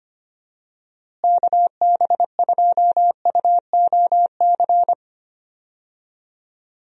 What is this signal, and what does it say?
Morse code, heard as a keyed tone, about 25 wpm, 700 Hz. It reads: KB2UOC
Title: USA Call Signs - 6 Characters - Repeated 2x - 25wpm